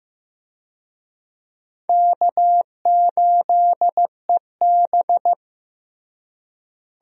Morse code, heard as a keyed tone, about 15 wpm, 700 Hz. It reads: K8EB
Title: USA Call Signs - 4 Characters - Repeated 2x - 15wpm